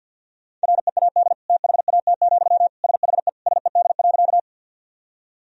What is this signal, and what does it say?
Morse code, heard as a keyed tone, about 50 wpm, 700 Hz. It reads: READ THAT, SHE SED.
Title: Sentences from Top 200 Words and Abbreviations 50wpm